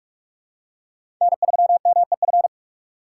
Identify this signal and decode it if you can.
Morse code, heard as a keyed tone, about 44 wpm, 700 Hz. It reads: N3MEF
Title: USA Call Signs - 4x Character Spacing - 45wpm